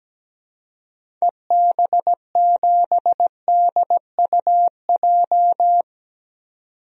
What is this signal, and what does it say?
Morse code, heard as a keyed tone, about 17 wpm, 700 Hz. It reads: EB7DUJ
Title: DXCC-100 Call Signs - 6 Characters - 17wpm